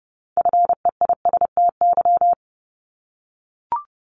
Morse code, heard as a keyed tone, about 30 wpm, 700 Hz. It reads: FEISTY
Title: Multi-letter Phonemes 30wpm